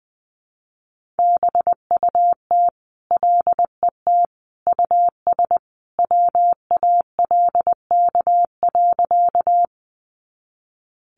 Morse code, heard as a keyed tone, about 20 wpm, 700 Hz. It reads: BUT LET US WALK.
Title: Sentences from Top 300 Words 20wpm